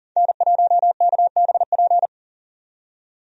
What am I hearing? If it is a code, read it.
Morse code, heard as a keyed tone, about 40 wpm, 700 Hz. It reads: N1KBP
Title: USA Call Signs - 5 Characters - 40wpm